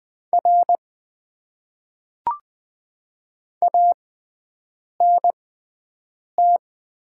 Morse code, heard as a keyed tone, about 20 wpm, spaced as for 7 wpm, 700 Hz. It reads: R ANT
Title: Top 90 QSO Elements - 6x Character Spacing - 20wpm